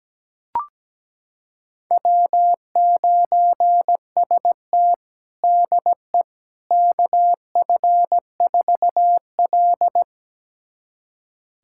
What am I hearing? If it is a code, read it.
Morse code, heard as a keyed tone, about 17 wpm, 700 Hz. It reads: W9ST DE KF4L
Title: USA Callsign DE Callsign - 4 Characters - 17wpm